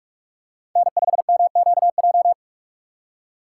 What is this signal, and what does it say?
Morse code, heard as a keyed tone, about 46 wpm, 700 Hz. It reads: N5MQJ
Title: USA Call Signs - 4x Character Spacing - 45wpm